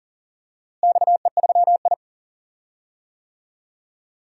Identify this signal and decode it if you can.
Morse code, heard as a keyed tone, about 40 wpm, 700 Hz. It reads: XE3I
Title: DXCC-100 Call Signs - 4 Characters - Repeated 3x - 40wpm